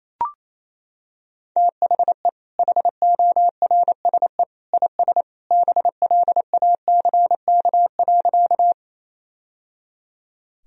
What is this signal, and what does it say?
Morse code, heard as a keyed tone, about 28 wpm, 700 Hz. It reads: THE HORSE IS BLACK.
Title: Sentences from Top 1000 Words - Review Part 2 - 28wpm